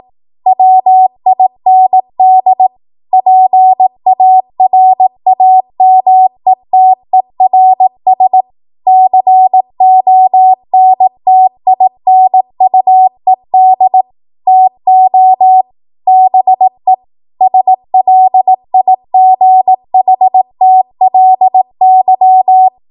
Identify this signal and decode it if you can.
Morse code, heard as a keyed tone, about 18 wpm, 750 Hz. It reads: WIND PARAMETERS CONTINUED TO BE SLIGHTLY
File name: Morse / 18WPM